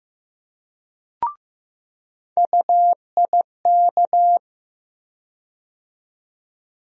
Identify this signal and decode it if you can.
Morse code, heard as a keyed tone, about 15 wpm, 700 Hz. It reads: UIK
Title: USA Call Sign Postfixes - 3 Letters 15wpm